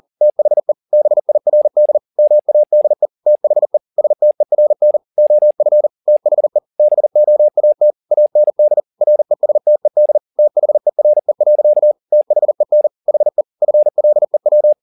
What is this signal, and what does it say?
Morse code, heard as a keyed tone, about 40 wpm, 600 Hz. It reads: THE BIRD MADE THE STERN OF THE BOAT AND RESTED THERE. THEN HE FLEW